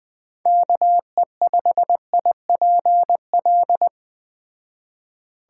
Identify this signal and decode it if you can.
Morse code, heard as a keyed tone, about 20 wpm, 700 Hz. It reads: KE5IPL
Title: Speed Racing - USA 2x3 Call Signs 17wpm